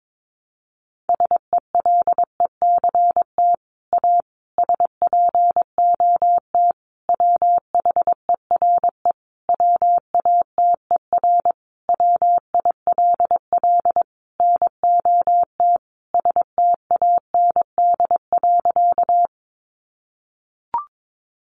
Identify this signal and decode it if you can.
Morse code, heard as a keyed tone, about 22 wpm, 700 Hz. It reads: SELECT A SPOT WHERE WATER WILL NOT STAND.
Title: Sentences from Top 1300 Words 22wpm